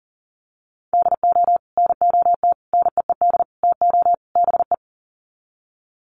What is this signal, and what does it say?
Morse code, heard as a keyed tone, about 40 wpm, 700 Hz. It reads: DO NOT NEED TO BE
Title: Sets of 5 Words Encore 40wpm